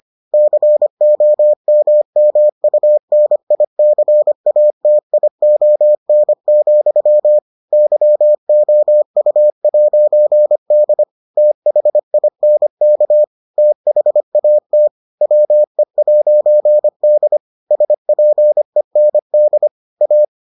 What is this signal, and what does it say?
Morse code, heard as a keyed tone, about 25 wpm, 600 Hz. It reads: COMMUNICATION, YOU'D THINK THAT WE'D SPEND A